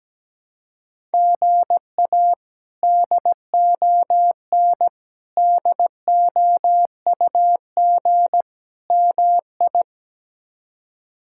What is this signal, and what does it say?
Morse code, heard as a keyed tone, about 17 wpm, 700 Hz. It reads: GA DON DOUG MI
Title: K1USN SST QSO Practice - 17wpm